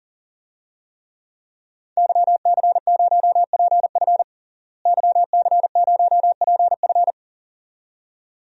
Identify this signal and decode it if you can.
Morse code, heard as a keyed tone, about 40 wpm, 700 Hz. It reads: YC0PF YC0PF
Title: DXCC-100 Call Signs - 5 Characters - Repeated 2x - 40wpm